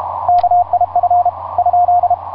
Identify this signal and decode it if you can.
Morse code, heard as a keyed tone, about 32 wpm, 710 Hz, with radio noise. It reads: KIF ?
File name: j38xx mw0cri etc 15m cw 18feb2010 1532